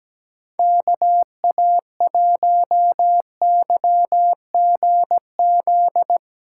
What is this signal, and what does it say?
Morse code, heard as a keyed tone, about 17 wpm, 700 Hz. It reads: KA1YGZ
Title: USA Callsign DE Callsign - 6 Characters - 17wpm